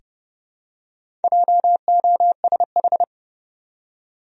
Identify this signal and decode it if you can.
Morse code, heard as a keyed tone, about 30 wpm, 700 Hz. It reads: JOSH
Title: USA Common Names - Max 4 Letters - 30wpm